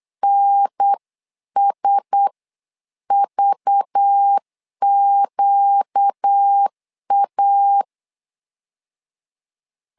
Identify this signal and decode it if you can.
Morse code, heard as a keyed tone, about 8 wpm, spaced as for 7 wpm, 800 Hz. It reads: NSVQA
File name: LEZIONI / 5 LBVC / 5 LBV 8wpm 2837